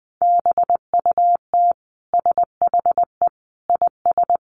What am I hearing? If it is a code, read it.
Morse code, heard as a keyed tone, about 20 wpm, 700 Hz. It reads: BUT SHE IS
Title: Sentences from Top 1500 Words 20wpm